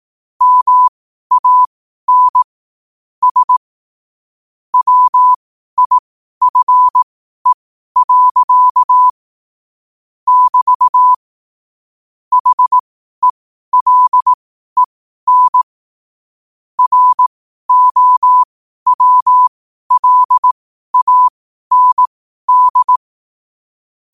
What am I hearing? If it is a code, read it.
Morse code, heard as a keyed tone, about 18 wpm, spaced as for 12 wpm, 1000 Hz. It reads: MAN S WIFE. <BT> HELEN ROWLAND
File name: qotdfiles / 12wpm